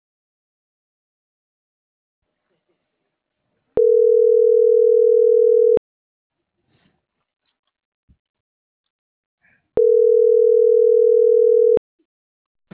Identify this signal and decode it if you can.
Ringback tone (North America), 440 + 480 Hz.